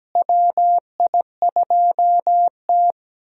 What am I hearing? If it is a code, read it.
Morse code, heard as a keyed tone, about 17 wpm, 700 Hz. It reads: WI2T
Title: USA Call Signs - 4 Characters - Repeated 2x - 17wpm